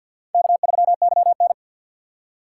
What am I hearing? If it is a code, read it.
Morse code, heard as a keyed tone, about 47 wpm, 700 Hz. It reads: K3YN
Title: USA Call Signs - 3x Character Spacing - 50wpm